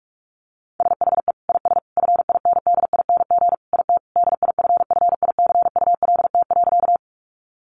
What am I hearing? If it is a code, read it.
Morse code, heard as a keyed tone, about 44 wpm, 700 Hz. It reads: SHE IS FINDING IT DIFFICULT.
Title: Sentences from Top 1100 Words 45wpm